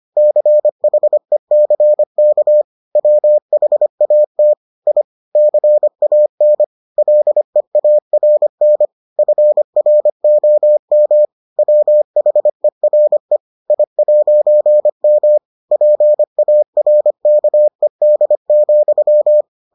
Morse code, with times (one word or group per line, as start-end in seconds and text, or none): 0.2-2.6 CHECK
3.0-4.5 WHAT
4.9-5.0 I
5.4-6.6 CAN
7.0-8.9 LEARN
9.2-11.3 FROM
11.6-13.4 WHERE
13.7-15.4 I'M
15.7-19.4 PARKED,